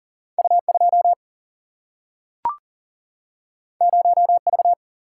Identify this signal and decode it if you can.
Morse code, heard as a keyed tone, about 39 wpm, 700 Hz. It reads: U 2 0 V
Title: Single-Letter Single-Number 40wpm